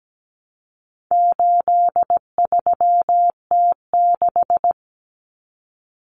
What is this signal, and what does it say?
Morse code, heard as a keyed tone, about 17 wpm, 700 Hz. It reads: 83T6
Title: Common Cut Numbers - 4-Digit 17wpm